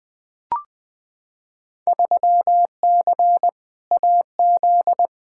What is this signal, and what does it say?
Morse code, heard as a keyed tone, about 20 wpm, 700 Hz. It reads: 3C AZ
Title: Field Day Exchanges Encore - 20wpm